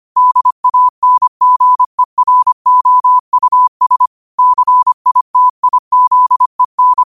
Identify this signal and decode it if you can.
Morse code, heard as a keyed tone, about 25 wpm, 1000 Hz. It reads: DANGEROUS CITIZEN
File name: qotdfiles / 25wpm